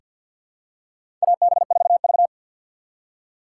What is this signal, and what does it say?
Morse code, heard as a keyed tone, about 49 wpm, 700 Hz. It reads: AB4V
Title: USA Call Signs - 4 Characters - 2x Character Spacing - 50wpm